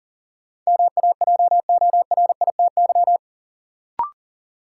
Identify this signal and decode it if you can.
Morse code, heard as a keyed tone, about 39 wpm, 700 Hz. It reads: MAJORITY
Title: Sets of 3 Words - New Word Drill - 40wpm